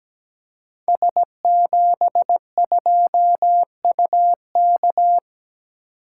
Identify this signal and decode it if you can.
Morse code, heard as a keyed tone, about 17 wpm, 700 Hz. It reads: S72UK
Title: Code Groups - 5 Letter:Numbers - 17wpm